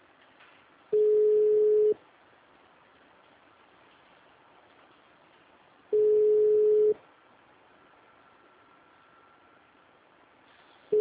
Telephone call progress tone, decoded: ringback tone

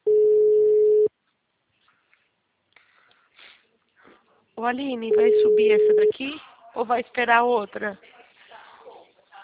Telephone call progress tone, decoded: ringback tone

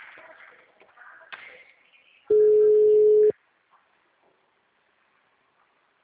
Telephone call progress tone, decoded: ringback tone

